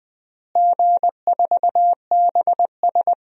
Morse code, heard as a keyed tone, about 21 wpm, 700 Hz. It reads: G4BS
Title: DXCC-100 Call Signs - 4 Characters - Repeated 3x - 20wpm